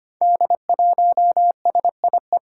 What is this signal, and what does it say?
Morse code, heard as a keyed tone, about 25 wpm, 700 Hz. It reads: D1SIE
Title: Five Challenging Letter-Number Combinations 25wpm